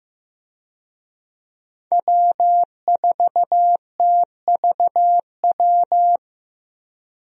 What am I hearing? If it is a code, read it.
Morse code, heard as a keyed tone, about 16 wpm, 700 Hz. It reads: W4TVW
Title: USA Call Signs - 5 Characters - Repeated 3x - 15wpm